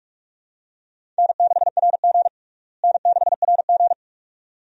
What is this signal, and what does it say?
Morse code, heard as a keyed tone, about 46 wpm, 700 Hz. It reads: N6RG N6RG
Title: USA Call Signs - 4 Characters - Repeated 2x - 45wpm